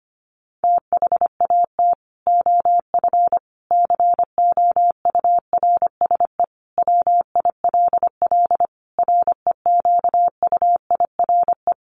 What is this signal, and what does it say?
Morse code, heard as a keyed tone, about 25 wpm, 700 Hz. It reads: THAT OF COURSE WILL REQUIRE
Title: Sentences from Top 1500 Words 25wpm